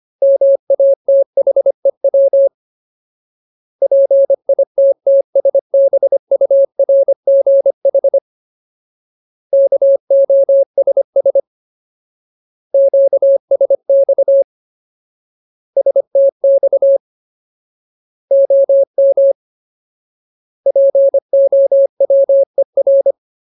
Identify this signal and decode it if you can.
Morse code, heard as a keyed tone, about 25 wpm, 550 Hz. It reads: MATHEW PITTSBURGH KOSS QSX STX OM POWER
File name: Ultimate Word List / 25WPM Ultimate